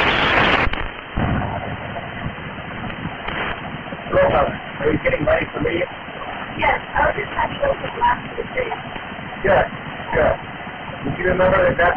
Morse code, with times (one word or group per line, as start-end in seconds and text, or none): none